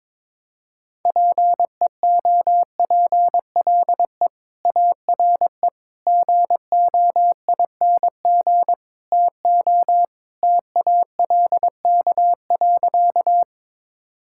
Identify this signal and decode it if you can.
Morse code, heard as a keyed tone, about 22 wpm, 700 Hz. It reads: PEOPLE ARE GOING TO TALK.
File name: practice / 22wpm / Sentences with 4-Word Phrases - 22wpm